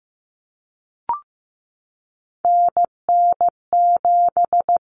Morse code, heard as a keyed tone, about 15 wpm, 700 Hz. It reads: NN7
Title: Common Cut Numbers - 3-Digit 15wpm